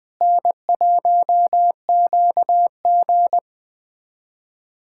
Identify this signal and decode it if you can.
Morse code, heard as a keyed tone, about 20 wpm, 700 Hz. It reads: N1QG
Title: USA Callsign DE Callsign - 4 Characters - 20wpm